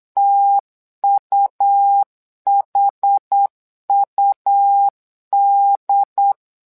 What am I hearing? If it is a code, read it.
Morse code, heard as a keyed tone, about 8 wpm, 800 Hz. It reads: TUHUD
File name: LEZIONI / 2 UHDN / 2 UHD 8wpm 2258